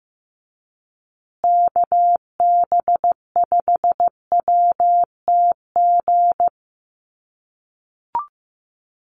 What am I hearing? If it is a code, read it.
Morse code, heard as a keyed tone, about 15 wpm, 700 Hz. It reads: KB5WTG
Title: USA Call Signs Repeated 3x 15wpm